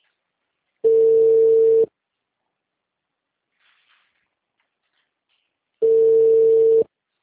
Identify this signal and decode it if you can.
Ringback tone (China, 450 Hz).
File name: out- -2042- - - .86034